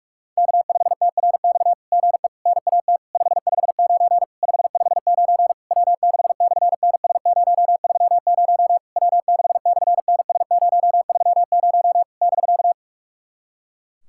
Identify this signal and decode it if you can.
Morse code, heard as a keyed tone, about 44 wpm, 700 Hz. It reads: K5TRX GE NAT 559 559 W6/NS030 W6/NS030 <BK>
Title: SOTA QSO Practice 45wpm